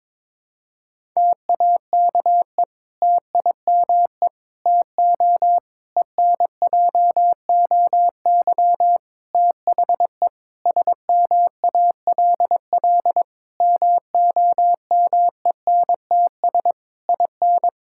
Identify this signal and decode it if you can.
Morse code, heard as a keyed tone, about 22 wpm, 700 Hz. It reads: TAKE TIME TO ENJOY THE SMALL MOMENTS IN
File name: Difficult Sentences with Hints - 22wpm